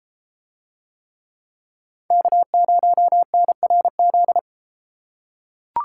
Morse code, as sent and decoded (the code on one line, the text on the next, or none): -.- ----- -. .-. --..
K0NRZ